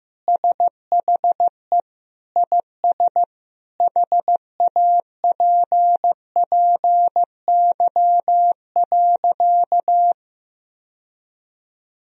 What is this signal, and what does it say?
Morse code, heard as a keyed tone, about 15 wpm, 700 Hz. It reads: SHE IS HAPPY.
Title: Sentences from Top 1000 Words - Review Part 3 - 15wpm